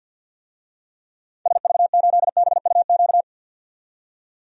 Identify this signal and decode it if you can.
Morse code, heard as a keyed tone, about 50 wpm, 700 Hz. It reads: SV8BUQ
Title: DXCC-100 Call Signs - 6 Characters - 50wpm